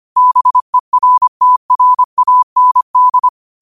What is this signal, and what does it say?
Morse code, heard as a keyed tone, about 25 wpm, 1000 Hz. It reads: BERTRAND